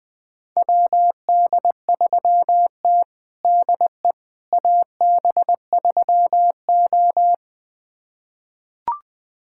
Morse code, heard as a keyed tone, about 20 wpm, 700 Hz. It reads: WD3T DE AB3O
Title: USA Callsign DE Callsign - 4 Characters - 20wpm